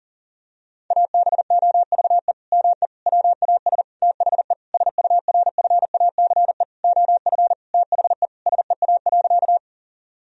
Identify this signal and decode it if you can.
Morse code, heard as a keyed tone, about 40 wpm, 700 Hz. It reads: ABOVE ME WAS THE SURFACE OF THE SEA.
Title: Sentences from Top 1000 Words - Review Part 3 - 40wpm